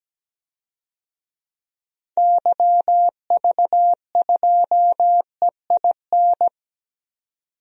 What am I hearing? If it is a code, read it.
Morse code, heard as a keyed tone, about 17 wpm, 700 Hz. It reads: YV2EIN
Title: DXCC-100 Call Signs - 6 Characters - 17wpm